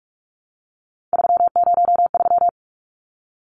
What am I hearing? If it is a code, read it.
Morse code, heard as a keyed tone, about 44 wpm, 700 Hz. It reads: 303
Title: Numbers - 3-Digit 45wpm